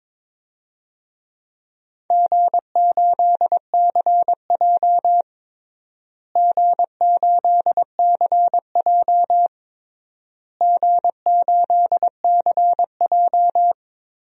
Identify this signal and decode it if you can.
Morse code, heard as a keyed tone, about 22 wpm, 700 Hz. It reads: G8CJ G8CJ G8CJ
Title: DXCC-100 Call Signs - 4 Characters - Repeated 3x - 22wpm